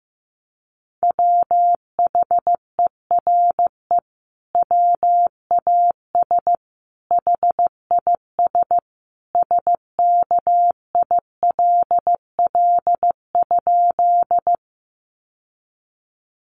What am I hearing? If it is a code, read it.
Morse code, heard as a keyed tone, about 15 wpm, 700 Hz. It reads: WHERE WAS HIS SKILL?
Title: Sentences from Top 1200 Words 15wpm